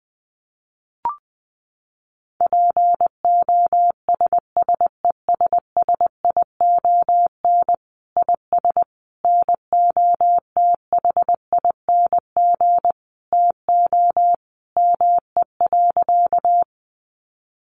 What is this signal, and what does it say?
Morse code, heard as a keyed tone, about 20 wpm, 700 Hz. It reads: POSSESSION IS NOTHING TO ME.
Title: Sentences from Top 1400 Words 20wpm